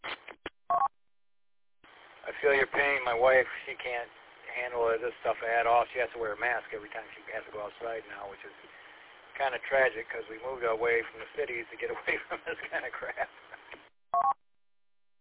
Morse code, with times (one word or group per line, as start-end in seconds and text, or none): none